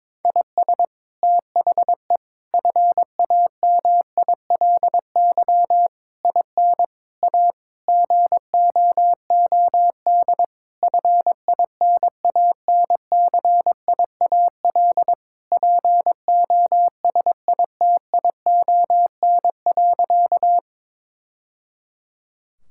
Morse code, heard as a keyed tone, about 22 wpm, 700 Hz. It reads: IS THE FAMILY IN A GOOD FINANCIAL POSITION.
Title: Sentences from Top 1000 Words - Review Part 5 - 22wpm